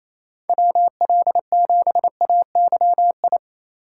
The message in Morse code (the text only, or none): WL7AYI